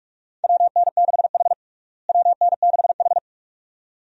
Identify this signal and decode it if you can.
Morse code, heard as a keyed tone, about 46 wpm, 700 Hz. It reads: WN6H WN6H
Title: USA Call Signs - 4 Characters - Repeated 2x - 45wpm